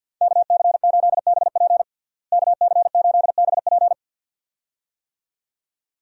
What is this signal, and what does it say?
Morse code, heard as a keyed tone, about 51 wpm, 700 Hz. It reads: KC8BP KC8BP
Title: USA Call Signs - 5 Characters - Repeated 2x - 50wpm